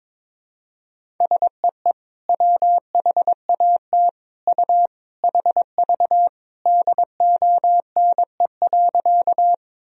Morse code, heard as a keyed tone, about 22 wpm, 700 Hz. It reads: SEE WHAT U HV DONE.
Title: Sentences from Top 500 Words and Abbreviations 22wpm